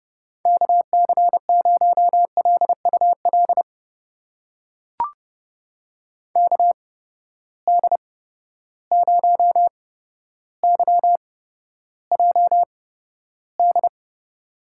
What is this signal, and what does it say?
Morse code, heard as a keyed tone, about 30 wpm, spaced as for 13 wpm, 700 Hz. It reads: KC0LUL KD0YJD
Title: USA Call Signs - 6 Characters - 8x Character Spacing - 30wpm